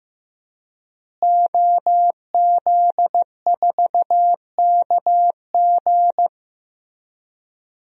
Morse code, heard as a keyed tone, about 15 wpm, 700 Hz. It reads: OZ4KG
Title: DXCC-100 Call Signs - 5 Characters - Repeated 3x - 15wpm